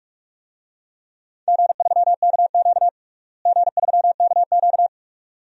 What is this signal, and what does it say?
Morse code, heard as a keyed tone, about 47 wpm, 700 Hz. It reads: G3KQ G3KQ
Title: DXCC-100 Call Signs - 4 Characters - Repeated 2x - 45wpm